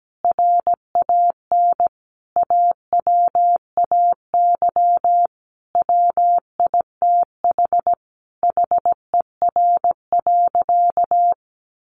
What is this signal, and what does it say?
Morse code, heard as a keyed tone, about 17 wpm, 700 Hz. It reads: RAN AWAY WITH HER.
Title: Sentences from Top 1500 Words 17wpm